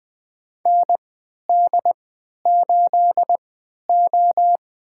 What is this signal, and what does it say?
Morse code, heard as a keyed tone, about 20 wpm, spaced as for 11 wpm, 700 Hz. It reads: ND8O
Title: USA Call Signs - 4 Characters - 3x Character Spacing - 20wpm